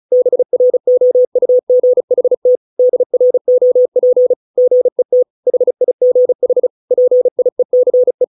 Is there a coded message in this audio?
Yes: Morse code from 0.1 to 8.2 s, heard as a keyed tone, about 35 wpm, 500 Hz. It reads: BROUGHT DROP GET HIGH PIECE